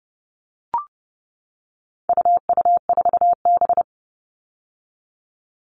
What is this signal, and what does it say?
Morse code, heard as a keyed tone, about 30 wpm, 700 Hz. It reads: UU4B
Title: All Cut Numbers - 4-Digit 30wpm